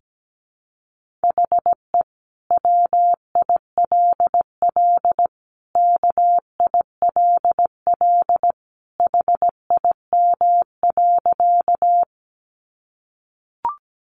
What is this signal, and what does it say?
Morse code, heard as a keyed tone, about 17 wpm, 700 Hz. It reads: HE WILL KILL HIM.
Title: Sentences from Top 1400 Words 17wpm